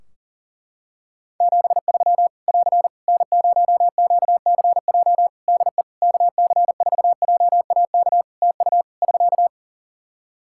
Morse code, heard as a keyed tone, about 40 wpm, 700 Hz. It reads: 73 <AR> N0QCJ DE KC4JAK TU <SK>
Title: K1USN SST QSO Practice 40wpm